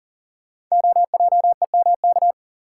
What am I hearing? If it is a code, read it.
Morse code, heard as a keyed tone, about 37 wpm, 700 Hz. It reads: OJEMK
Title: Code Groups - 5 Letters - 40wpm